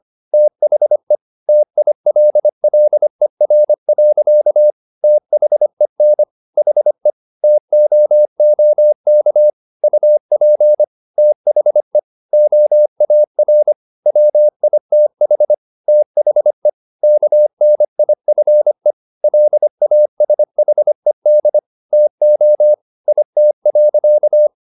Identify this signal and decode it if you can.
Morse code, heard as a keyed tone, about 25 wpm, 600 Hz. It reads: THE TILLER. THEN HE TOOK UP THE OAR WITH THE KNIFE LASHED TO IT.